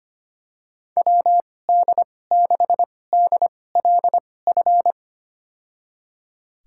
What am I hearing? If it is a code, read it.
Morse code, heard as a keyed tone, about 25 wpm, spaced as for 18 wpm, 700 Hz. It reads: WD6DLF
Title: USA Call Signs - 6 Characters - 2x Character Spacing - 25wpm